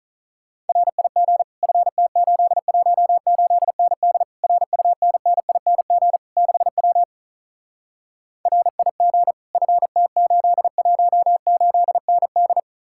Morse code, heard as a keyed tone, about 48 wpm, 700 Hz. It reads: RIG FT818ND RUNNING 6W RIG FT818ND